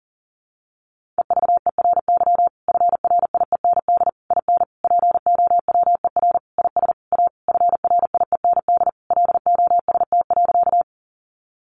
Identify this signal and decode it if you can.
Morse code, heard as a keyed tone, about 40 wpm, 700 Hz. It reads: EVERY FRIEND IN POWER IS A FRIEND LOST.
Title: Sentences from Top 1400 Words 40wpm